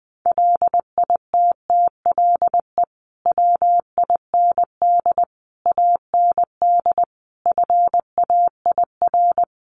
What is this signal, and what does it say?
Morse code, heard as a keyed tone, about 20 wpm, 700 Hz. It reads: LITTLE WIND AND FAIR
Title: Sentences from Top 1300 Words 20wpm